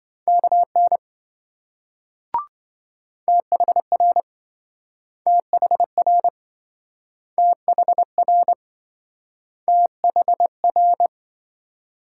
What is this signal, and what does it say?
Morse code, heard as a keyed tone, about 26 wpm, 700 Hz. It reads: KN THR THR THR THR